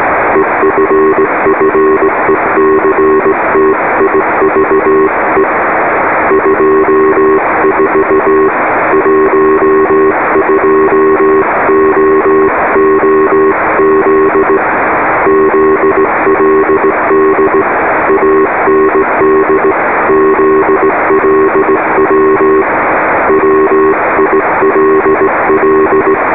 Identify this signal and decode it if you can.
Morse code, heard as a keyed tone, about 18 wpm, 390 Hz, with radio noise. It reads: EFFECTIVE 24 12OOZ ZLD AND ZLW WILL